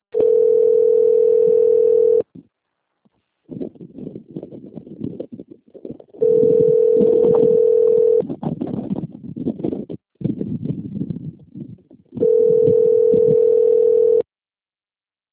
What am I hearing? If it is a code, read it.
Ringback tone (North America), 440 + 480 Hz.
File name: rg-1600-unknown- - - .12744